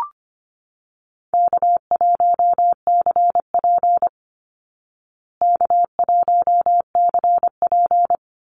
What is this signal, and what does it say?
Morse code, heard as a keyed tone, about 25 wpm, 700 Hz. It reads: K1CP K1CP
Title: Easy Call Signs Repeated 2x - 25wpm